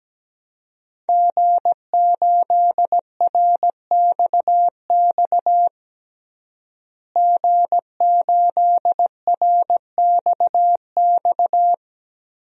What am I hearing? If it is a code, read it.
Morse code, heard as a keyed tone, about 17 wpm, 700 Hz. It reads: G8RXX G8RXX
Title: DXCC-100 Call Signs - 5 Characters - Repeated 2x - 17wpm